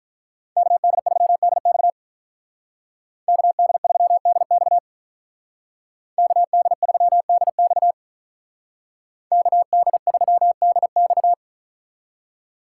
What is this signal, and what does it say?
Morse code, heard as a keyed tone, about 46 wpm, 700 Hz. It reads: KD3DX KD3DX KD3DX KD3DX